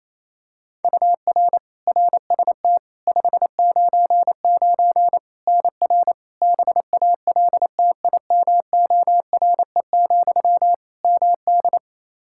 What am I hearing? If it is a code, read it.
Morse code, heard as a keyed tone, about 28 wpm, 700 Hz. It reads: UR RST 599 NR BALTIMORE, MD
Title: Practice QSO - Exchange 1 - Beginning - 28wpm